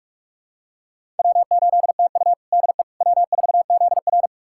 Morse code, heard as a keyed tone, about 41 wpm, 700 Hz. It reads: W8TU DE W4ZR
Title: USA Callsign DE Callsign - 4 Characters - 45wpm